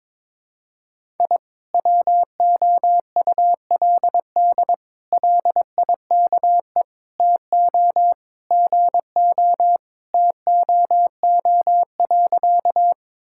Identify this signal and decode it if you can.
Morse code, heard as a keyed tone, about 22 wpm, 700 Hz. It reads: I WOULD LIKE TO GO TOO.
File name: practice / 22wpm / Sentences with 3-Word Phrases - 22wpm